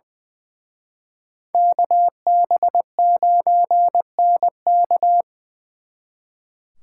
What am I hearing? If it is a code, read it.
Morse code, heard as a keyed tone, about 24 wpm, 700 Hz. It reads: KB9NK
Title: Speed Racing - USA 2x2 Call Signs 20wpm